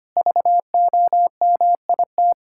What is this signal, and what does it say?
Morse code, heard as a keyed tone, about 25 wpm, 700 Hz. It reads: VOMIT
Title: Pangram Challenge v2 - 25wpm